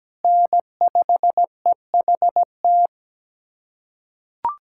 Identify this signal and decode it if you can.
Morse code, heard as a keyed tone, about 17 wpm, 700 Hz. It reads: N5EHT
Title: USA Callsign DE Callsign - 5 Characters - 17wpm